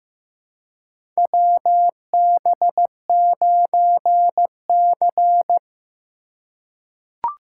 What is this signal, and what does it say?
Morse code, heard as a keyed tone, about 15 wpm, 700 Hz. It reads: WB9C
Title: USA Call Signs - 4 Characters - 3x Character Spacing - 15wpm